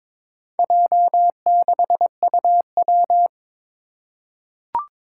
Morse code, heard as a keyed tone, about 22 wpm, 700 Hz. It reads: J6UW
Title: Four Challenging Letter-Number Combinations 22wpm